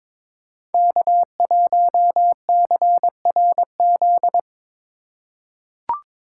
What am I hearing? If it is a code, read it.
Morse code, heard as a keyed tone, about 22 wpm, 700 Hz. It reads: K1CRZ
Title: USA Call Signs - 5 Characters - 3x Character Spacing - 22wpm